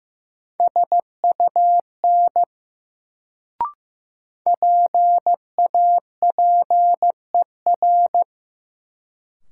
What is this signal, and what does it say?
Morse code, heard as a keyed tone, about 16 wpm, 700 Hz. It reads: SUN PAPER
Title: Sentences from Top 300 Words - New Word Drill - 15wpm